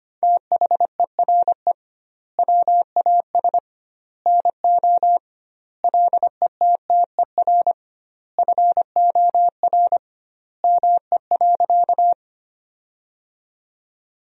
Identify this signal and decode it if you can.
Morse code, heard as a keyed tone, about 25 wpm, 700 Hz. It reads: THERE WAS NO LETTER FOR ME.
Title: 2x Word-Spacing - Sentences from Top 300 Words - 25wpm